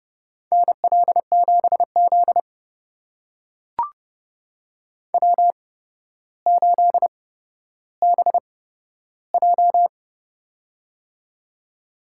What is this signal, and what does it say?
Morse code, heard as a keyed tone, about 30 wpm, spaced as for 13 wpm, 700 Hz. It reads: NL7Z W8BJ
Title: USA Call Signs - 4 Characters - 8x Character Spacing - 30wpm